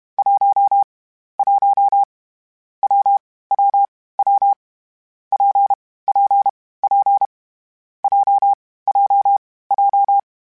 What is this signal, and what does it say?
Morse code, heard as a keyed tone, about 32 wpm, 800 Hz. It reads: 1 1 W W W P P P J J J
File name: Difficult Combos / Difficult Combos-mp3s / WPJ WPJ1 @30